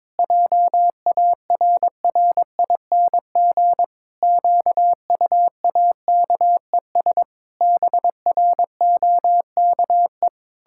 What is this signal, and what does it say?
Morse code, heard as a keyed tone, about 22 wpm, 700 Hz. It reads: JARRING QUAKES BROKE
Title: Pangram Challenge v2 - 22wpm